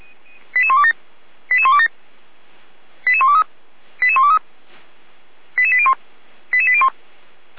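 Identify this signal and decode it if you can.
Five-tone selective call (ZVEI), heard as quick runs of five tones, tones of about 70 ms. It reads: 88127 88127 88123 88123 88801 88801
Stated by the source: ZVEI five-tone selective call received on an SDR by feuerwehr-orlamuende (Orlamünde fire brigade).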